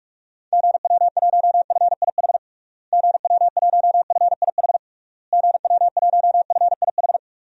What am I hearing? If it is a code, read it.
Morse code, heard as a keyed tone, about 46 wpm, 700 Hz. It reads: GW1FIH GW1FIH GW1FIH
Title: DXCC-100 Call Signs - 6 Characters - Repeated 3x - 45wpm